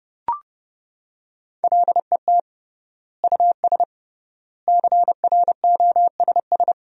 Morse code, heard as a keyed tone, about 30 wpm, 700 Hz. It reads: LET US CROSS
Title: 3x Word-Spacing - Sentences from Top 300 Words - 30wpm